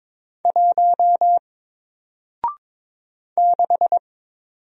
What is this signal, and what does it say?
Morse code, heard as a keyed tone, about 22 wpm, 700 Hz. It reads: 1 6
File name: Single-Letter-Number-22wpm